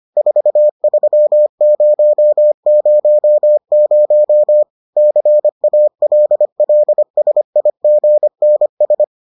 Morse code, heard as a keyed tone, about 25 wpm, 600 Hz. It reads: 43000 CALLSIGNS